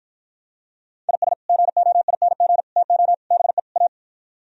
Morse code, heard as a keyed tone, about 53 wpm, 700 Hz. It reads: IS GOING TO BE A